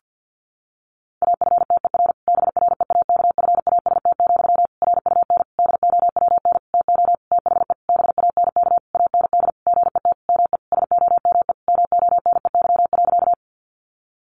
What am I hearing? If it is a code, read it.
Morse code, heard as a keyed tone, about 50 wpm, 700 Hz. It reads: AFTER BREAKFAST, RUN DOWN TO THE BANK AND GET ME SOME MONEY.
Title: Sentences from Top 1500 Words 50wpm